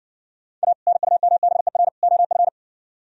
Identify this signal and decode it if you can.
Morse code, heard as a keyed tone, about 69 wpm, 700 Hz. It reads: A NUMBER OF